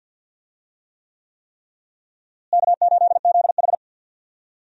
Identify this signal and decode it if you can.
Morse code, heard as a keyed tone, about 52 wpm, 700 Hz. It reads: K8ZH
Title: USA Call Signs - 4 Characters - Repeated 3x - 50wpm